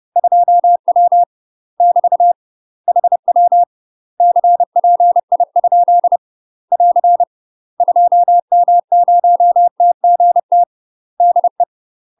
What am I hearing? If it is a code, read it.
Morse code, heard as a keyed tone, about 30 wpm, 700 Hz. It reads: 2W <BT> HW CPI ? <AR> 2M0TGT DE
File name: UK MASTER QSOs TRAINING / 30WPM / UK MASTER QSOs 0013